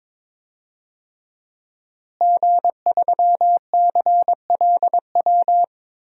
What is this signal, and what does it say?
Morse code, heard as a keyed tone, about 22 wpm, 700 Hz. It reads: G3CLW
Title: DXCC-100 Call Signs - 5 Characters - Repeated 3x - 22wpm